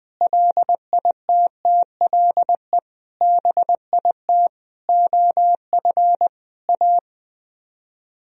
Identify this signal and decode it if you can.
Morse code, heard as a keyed tone, about 20 wpm, 700 Hz. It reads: LITTLE BIT OF A
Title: Speed-Racing - Sets of 5 Words 17wpm